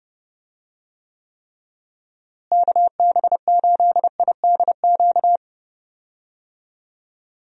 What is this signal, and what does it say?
Morse code, heard as a keyed tone, about 30 wpm, 700 Hz. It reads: KB8IDQ
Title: USA Call Signs - 6 Characters - Repeated 3x - 30wpm